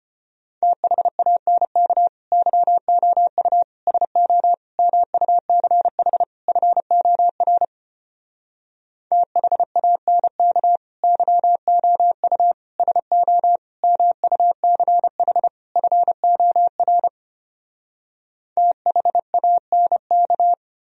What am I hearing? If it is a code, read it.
Morse code, heard as a keyed tone, about 33 wpm, 700 Hz. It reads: THANK YOU SO MUCH FOR THANK YOU SO MUCH FOR THANK